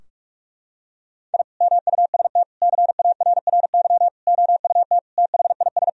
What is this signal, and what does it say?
Morse code, heard as a keyed tone, about 46 wpm, 700 Hz. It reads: I MUST CARRY OUT THIS
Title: Sentences from Top 1000 Words - Review Part 2 - 45wpm